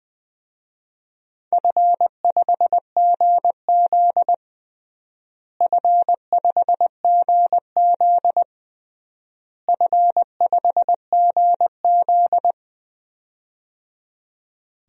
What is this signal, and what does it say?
Morse code, heard as a keyed tone, about 20 wpm, 700 Hz. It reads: F5GZ F5GZ F5GZ
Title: DXCC-100 Call Signs - 4 Characters - Repeated 3x - 20wpm